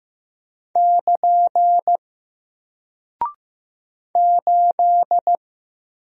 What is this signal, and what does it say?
Morse code, heard as a keyed tone, about 15 wpm, 700 Hz. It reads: <KN> 8
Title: Single Character (Letter:Number:Punctuation:Prosign) 15wpm